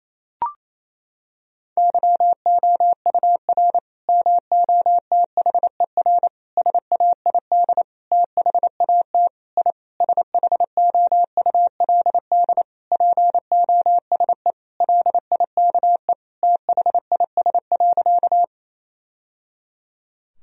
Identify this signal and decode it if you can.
Morse code, heard as a keyed tone, about 28 wpm, 700 Hz. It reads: YOUR MOTHER SAID THAT I SHOULD POSE LIKE THIS.
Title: Sentences from Top 1000 Words - Review Part 2 - 28wpm